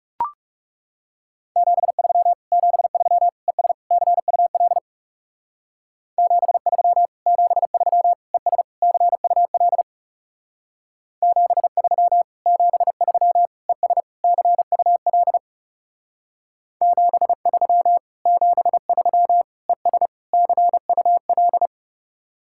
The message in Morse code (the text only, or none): 73 73 ES CUL 73 73 ES CUL 73 73 ES CUL 73 73 ES CUL